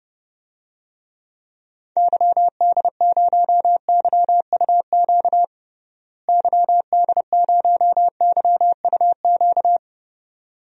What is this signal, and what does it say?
Morse code, heard as a keyed tone, about 30 wpm, 700 Hz. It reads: YD0YUQ YD0YUQ
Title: DXCC-100 Call Signs - 6 Characters - Repeated 2x - 30wpm